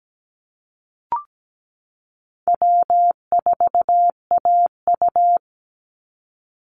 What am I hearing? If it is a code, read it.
Morse code, heard as a keyed tone, about 17 wpm, 700 Hz. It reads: W4AU
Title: Call Signs - 17wpm